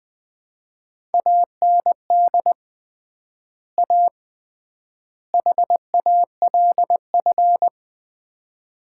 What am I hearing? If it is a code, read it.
Morse code, heard as a keyed tone, about 21 wpm, 700 Hz. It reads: AND A HALF
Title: 3x Word-Spacing - Sets of 3 Words - 20wpm